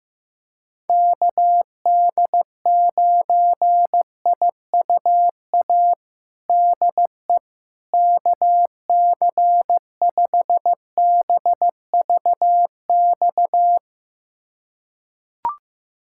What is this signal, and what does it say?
Morse code, heard as a keyed tone, about 15 wpm, 700 Hz. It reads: KD9IUA DE KC5BVX
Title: USA Callsign DE Callsign 15wpm